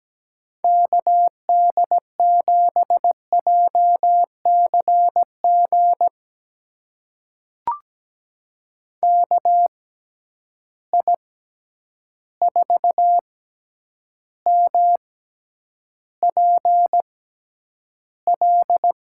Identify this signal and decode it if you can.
Morse code, heard as a keyed tone, about 17 wpm, spaced as for 9 wpm, 700 Hz. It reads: KD7JCG KI4MPL
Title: USA Call Signs - 6x Character Spacing - 17wpm